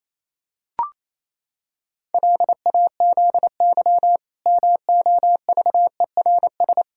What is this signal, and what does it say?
Morse code, heard as a keyed tone, about 28 wpm, 700 Hz. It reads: LAZY MOVERS
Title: Pangram Challenge v2 - 28wpm